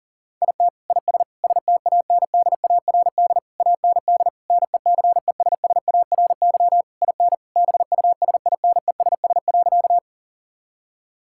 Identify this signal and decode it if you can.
Morse code, heard as a keyed tone, about 40 wpm, 700 Hz. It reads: IT IS STANDARD AND NECESSARY IN BUSINESS.
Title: Sentences from Top 1000 Words - Review Part 6 - 40wpm